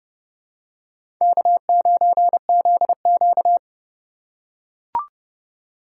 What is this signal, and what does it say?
Morse code, heard as a keyed tone, about 32 wpm, 700 Hz. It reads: K9ZQ
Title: USA Call Signs - 4 Characters - 4x Character Spacing - 30wpm